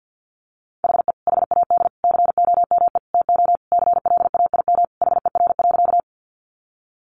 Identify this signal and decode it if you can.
Morse code, heard as a keyed tone, about 48 wpm, 700 Hz. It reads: HE HAD COME TO CLAIM HER.
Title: Sentences from Top 1500 Words 50wpm